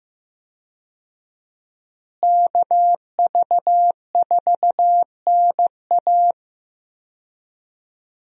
Morse code, heard as a keyed tone, about 15 wpm, 700 Hz. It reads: KV4NA
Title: USA Call Signs - 5 Characters - Repeated 2x - 15wpm